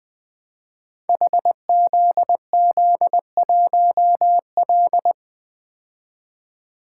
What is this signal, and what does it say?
Morse code, heard as a keyed tone, about 20 wpm, 700 Hz. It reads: HZZ1L
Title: Code Groups - 5 Letter:Numbers - 20wpm